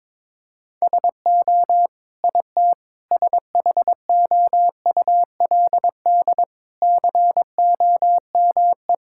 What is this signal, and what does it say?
Morse code, heard as a keyed tone, about 22 wpm, 700 Hz. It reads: SO IT SHOULD COME
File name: practice / 22wpm / Sets of 8 Words - 22wpm